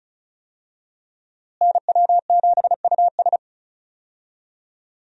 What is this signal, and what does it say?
Morse code, heard as a keyed tone, about 34 wpm, 700 Hz. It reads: NW7US
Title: USA Call Signs - 5 Characters - Repeated 3x - 35wpm